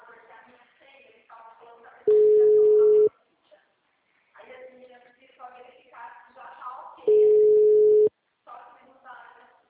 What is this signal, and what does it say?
Ringback tone (Europe, 425 Hz), with speech.
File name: out- -2001- - - .428